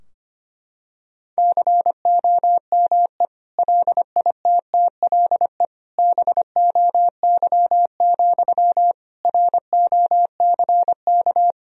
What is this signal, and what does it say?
Morse code, heard as a keyed tone, about 25 wpm, 700 Hz. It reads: COME LITTLE BOY, ROCK
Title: Sentences from Top 1000 Words - Review Part 2 - 25wpm